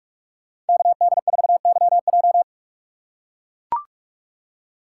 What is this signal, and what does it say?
Morse code, heard as a keyed tone, about 44 wpm, 700 Hz. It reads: KD4YJ